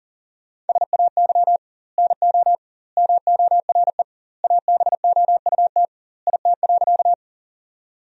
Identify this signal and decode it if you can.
Morse code, heard as a keyed tone, about 37 wpm, 700 Hz. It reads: SAY NO MORE ABOUT IT.
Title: 2x Word-Spacing - Sentences from Top 200 Words - 40wpm